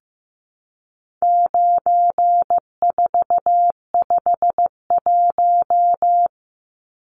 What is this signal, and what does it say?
Morse code, heard as a keyed tone, about 15 wpm, 700 Hz. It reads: 9451
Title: Numbers - 4-Digit 15wpm